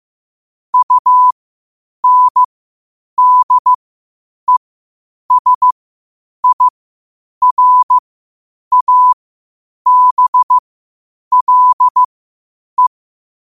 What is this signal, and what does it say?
Morse code, heard as a keyed tone, about 15 wpm, spaced as for 8 wpm, 1000 Hz. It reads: UNDESIRABLE
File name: qotdfiles / 8wpm